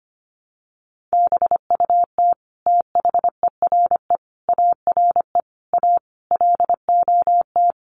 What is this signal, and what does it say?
Morse code, heard as a keyed tone, about 25 wpm, 700 Hz. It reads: BUT THERE ARE A LOT
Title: Sets of 5 Words Encore2 25wpm